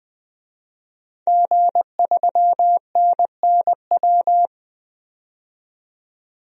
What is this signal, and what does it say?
Morse code, heard as a keyed tone, about 20 wpm, 700 Hz. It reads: G3NNW
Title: DXCC-100 Call Signs - 5 Characters - 20wpm